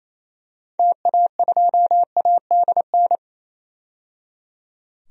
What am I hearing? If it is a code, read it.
Morse code, heard as a keyed tone, about 28 wpm, 700 Hz. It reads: TA2ADN
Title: DXCC-100 Call Signs - 6 Characters - 28wpm